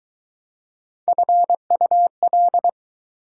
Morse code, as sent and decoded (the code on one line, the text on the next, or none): ..-. ..- .-..
FUL